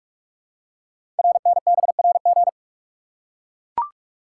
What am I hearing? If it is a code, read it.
Morse code, heard as a keyed tone, about 44 wpm, 690 Hz. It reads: RNBRG
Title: Code Groups - 5 Letters - 45wpm